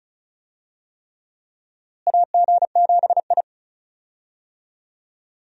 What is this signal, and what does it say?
Morse code, heard as a keyed tone, about 36 wpm, 700 Hz. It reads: AG7I
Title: USA Call Signs - 4 Characters - Repeated 3x - 35wpm